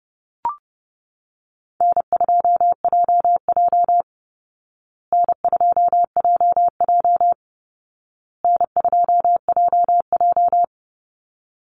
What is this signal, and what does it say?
Morse code, heard as a keyed tone, about 30 wpm, 700 Hz. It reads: N2JJ N2JJ N2JJ
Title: Easy Call Signs Repeated 3x - 30wpm